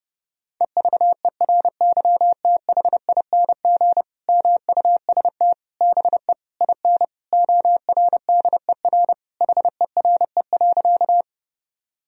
Morse code, heard as a keyed tone, about 30 wpm, 700 Hz. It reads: EVERYTHING MUST BE IN ORDER HERE.
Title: Sentences from Top 1000 Words - Review Part 2 - 30wpm